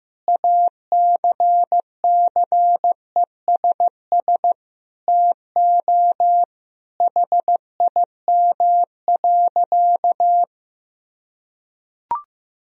Morse code, heard as a keyed tone, about 15 wpm, 700 Hz. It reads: ACCESS TO HIM.
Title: Sentences from Top 1000 Words - Review Part 4 - 15wpm